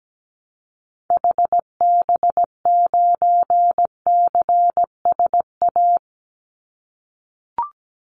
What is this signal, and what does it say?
Morse code, heard as a keyed tone, about 17 wpm, 700 Hz. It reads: HB9CSA
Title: Call Signs - 17wpm